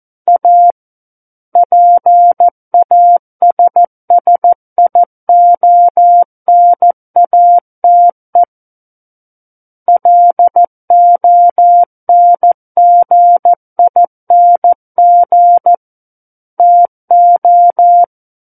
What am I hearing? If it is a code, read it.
Morse code, heard as a keyed tone, about 14 wpm, 700 Hz. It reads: A PASSIONATE LONGING TO
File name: Book One / 05-Chapter 5 - 14 WPM